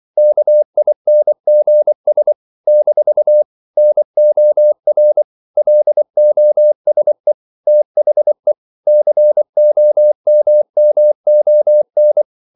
Morse code, heard as a keyed tone, about 24 wpm, 600 Hz. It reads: KINGS - NOR LOSE THE COMMON
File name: CW / Practice / If24